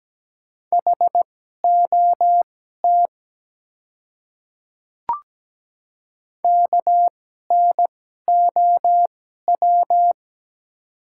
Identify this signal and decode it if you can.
Morse code, heard as a keyed tone, about 17 wpm, spaced as for 12 wpm, 700 Hz. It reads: HOT KNOW
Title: 2x Character Spacing - Top 100 Words - 17wpm